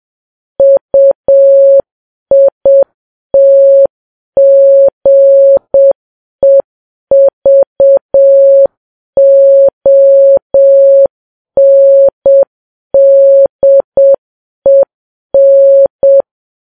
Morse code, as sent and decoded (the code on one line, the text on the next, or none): ..- .. - --. ...- --- -. -.. -.
UITGEVONDEN